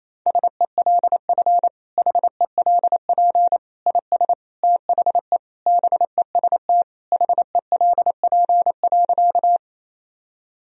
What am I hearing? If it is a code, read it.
Morse code, heard as a keyed tone, about 28 wpm, 700 Hz. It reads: SELF HELP IS THE BEST HELP.
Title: Sentences from Top 1000 Words - Review Part 1 - 28wpm